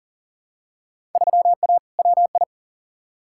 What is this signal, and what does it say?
Morse code, heard as a keyed tone, about 39 wpm, 700 Hz. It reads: 3A WI
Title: Field Day Exchanges - 40wpm